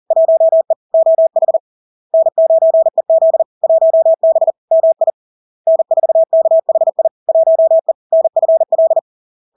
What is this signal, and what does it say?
Morse code, heard as a keyed tone, about 40 wpm, 650 Hz. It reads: 1E OH N9EZ 1B MI N4KHI 1E NFL
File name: LARK-FD-2022-Log-in-CW-40wpm-650-Hz